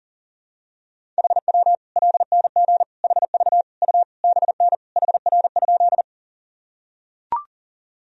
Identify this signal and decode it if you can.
Morse code, heard as a keyed tone, about 40 wpm, 700 Hz. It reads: HW LNG HV U BN HR?
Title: Sentences from Top 200 Words and Abbreviations 40wpm